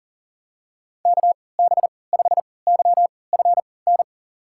Morse code, heard as a keyed tone, about 40 wpm, spaced as for 23 wpm, 700 Hz. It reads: KB5YFN
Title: USA Call Signs - 6 Characters - 3x Character Spacing - 40wpm